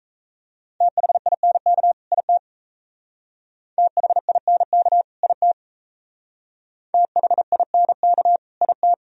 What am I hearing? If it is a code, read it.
Morse code, heard as a keyed tone, about 35 wpm, 700 Hz. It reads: THINK IT THINK IT THINK IT
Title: Speed-Racing - Sets of 2 Words 28wpm